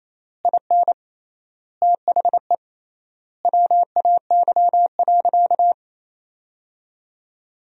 Morse code, heard as a keyed tone, about 28 wpm, 700 Hz. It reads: IN THE WAY.
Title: 3x Word-Spacing - Sentences from Top 400 Words - 28wpm